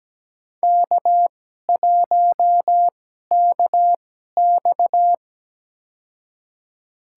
Morse code, heard as a keyed tone, about 17 wpm, spaced as for 12 wpm, 700 Hz. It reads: K1KX
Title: USA Call Signs - 4 Characters - 2x Character Spacing - 17wpm